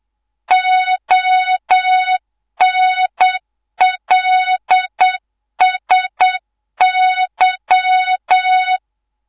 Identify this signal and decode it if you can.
Morse code, heard as a keyed tone, about 8 wpm, 3700 Hz, with noise. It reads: ONLSY
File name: Morze-HA2MN / hangp / 2 03 GYAKORLAT